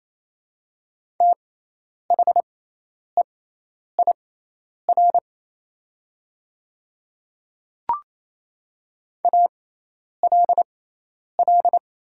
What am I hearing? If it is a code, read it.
Morse code, heard as a keyed tone, about 27 wpm, spaced as for 10 wpm, 700 Hz. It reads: THEIR ALL
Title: 6x Character Spacing - Top 100 Words - 28wpm